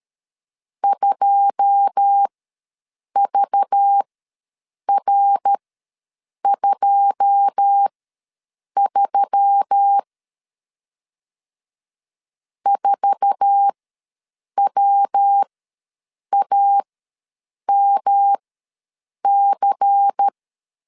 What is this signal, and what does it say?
Morse code, heard as a keyed tone, about 13 wpm, spaced as for 7 wpm, 800 Hz. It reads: 2VR23 4WAMC
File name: LEZIONI / 9 2 ws 12wpm 3406